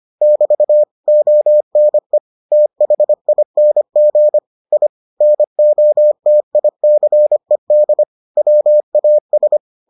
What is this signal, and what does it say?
Morse code, heard as a keyed tone, about 25 wpm, 600 Hz. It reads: <BT> ONE THING I NOTICED WAS